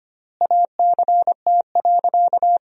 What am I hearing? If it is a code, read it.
Morse code, heard as a keyed tone, about 25 wpm, 700 Hz. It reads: ACT.
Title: 4x Word-Spacing - Sentences from Top 200 Words - 25wpm